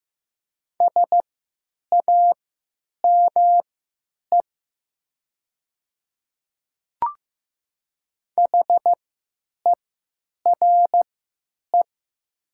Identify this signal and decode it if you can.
Morse code, heard as a keyed tone, about 15 wpm, spaced as for 9 wpm, 700 Hz. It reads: SAME HERE
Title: 3x Character Spacing - Top 200 Words - 15wpm